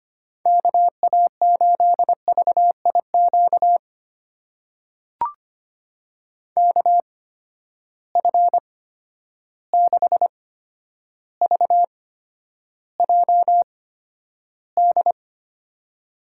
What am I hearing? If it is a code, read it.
Morse code, heard as a keyed tone, about 25 wpm, spaced as for 11 wpm, 700 Hz. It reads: KA8VIQ KF6VJD
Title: USA Call Signs - 6 Characters - 8x Character Spacing - 25wpm